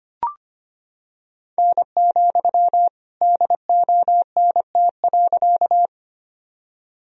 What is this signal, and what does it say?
Morse code, heard as a keyed tone, about 25 wpm, 700 Hz. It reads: N, DONT.
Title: Sentences from Top 300 Words and Abbreviations 25wpm